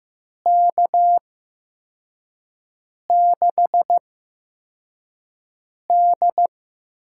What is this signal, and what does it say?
Morse code, heard as a keyed tone, about 15 wpm, spaced as for 4 wpm, 700 Hz. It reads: K6D
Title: USA Call Signs - 5 Characters - 8x Character Spacing - 15wpm